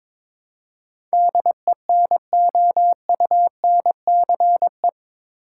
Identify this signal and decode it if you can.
Morse code, heard as a keyed tone, about 22 wpm, 700 Hz. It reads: DENOUNCE
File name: practice / 22wpm / Eight Letter Words - 22wpm